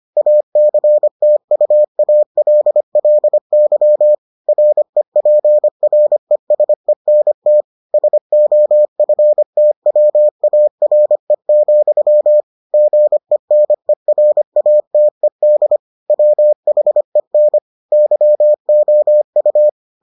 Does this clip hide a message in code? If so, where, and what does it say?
Yes: Morse code from 0.2 to 19.7 s, heard as a keyed tone, about 25 wpm, 600 Hz. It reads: ACTUALLY REPRESENT SOFTWARE, GENERATED WHEN YOU